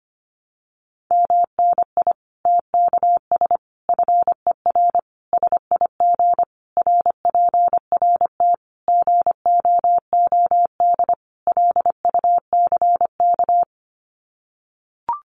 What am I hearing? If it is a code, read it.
Morse code, heard as a keyed tone, about 25 wpm, 700 Hz. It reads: MNI TKS FER SIG RPRT GOOD LUCK
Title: Common QSO Phrases 25wpm